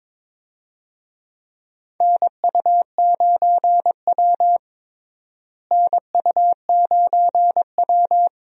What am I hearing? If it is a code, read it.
Morse code, heard as a keyed tone, about 22 wpm, 700 Hz. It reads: NU9W NU9W
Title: USA Call Signs - 4 Characters - Repeated 3x - 22wpm